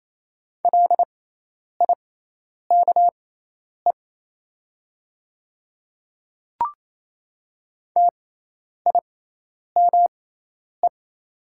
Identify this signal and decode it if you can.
Morse code, heard as a keyed tone, about 27 wpm, spaced as for 10 wpm, 700 Hz. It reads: LIKE TIME
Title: 6x Character Spacing - Top 100 Words - 28wpm